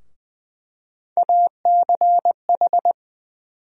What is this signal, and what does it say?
Morse code, heard as a keyed tone, about 20 wpm, 700 Hz. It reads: ACH
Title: USA Call Sign Postfixes - 3 Letters 20wpm